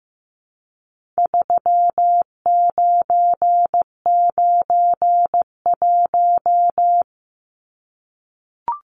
Morse code, heard as a keyed tone, about 15 wpm, 700 Hz. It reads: 3991
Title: Numbers - 4-Digit 15wpm